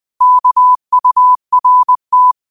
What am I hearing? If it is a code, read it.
Morse code, heard as a keyed tone, about 20 wpm, 1000 Hz. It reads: KURT
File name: qotdfiles / 20wpm